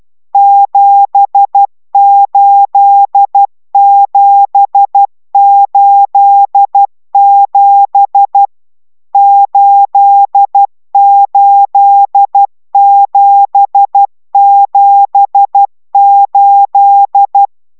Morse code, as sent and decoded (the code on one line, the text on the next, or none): --... ---.. --... ---.. --... / ---.. ---.. --... --... ---..
78787 88778